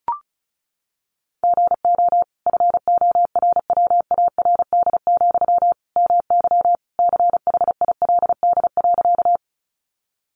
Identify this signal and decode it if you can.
Morse code, heard as a keyed tone, about 34 wpm, 700 Hz. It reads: GO FORWARD, MY CHILD.
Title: Sentences from Top 1300 Words 35wpm